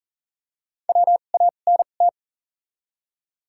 Morse code, heard as a keyed tone, about 40 wpm, spaced as for 29 wpm, 700 Hz. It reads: WANT
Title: 2x Character Spacing - Top 300 Words - 40wpm